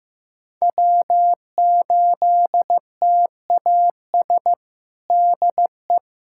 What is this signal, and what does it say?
Morse code, heard as a keyed tone, about 15 wpm, 700 Hz. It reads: W8TAS DE
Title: USA Callsign DE Callsign 15wpm